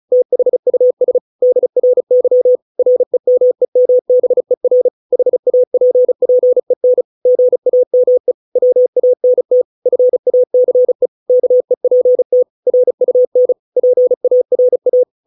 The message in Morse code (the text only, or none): THUS DRY REMEMBER HAPPEN GAME WANT FACE KEPT RUN PARA